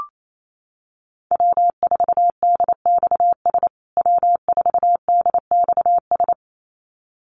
Morse code, heard as a keyed tone, about 28 wpm, 700 Hz. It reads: W4DXS W4DXS
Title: USA Call Signs Repeated 2x 28wpm